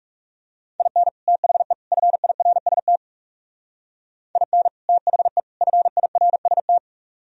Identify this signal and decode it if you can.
Morse code, heard as a keyed tone, about 38 wpm, 700 Hz. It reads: IN THE FIRST IN THE FIRST